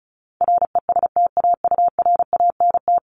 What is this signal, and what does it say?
Morse code, heard as a keyed tone, about 34 wpm, 700 Hz. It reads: RESTAURANT
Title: Top 1751-2000 Words 35wpm